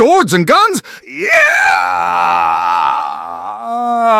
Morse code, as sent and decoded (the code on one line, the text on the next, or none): none